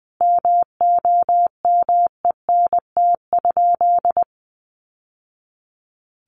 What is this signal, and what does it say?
Morse code, heard as a keyed tone, about 20 wpm, 700 Hz. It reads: MOMENT?
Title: Sentences from Top 1000 Words 20wpm